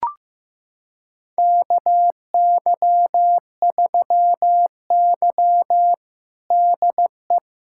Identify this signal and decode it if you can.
Morse code, heard as a keyed tone, about 15 wpm, 700 Hz. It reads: KY3Y DE
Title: USA Callsign DE Callsign - 4 Characters - 15wpm